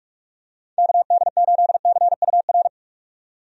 Morse code, heard as a keyed tone, about 45 wpm, 700 Hz. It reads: KD8CUR
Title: USA Call Signs - 6 Characters - 45wpm